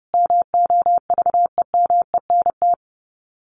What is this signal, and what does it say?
Morse code, heard as a keyed tone, about 30 wpm, 700 Hz. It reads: MOVEMENT
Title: Sets of 5 Words Encore2 30wpm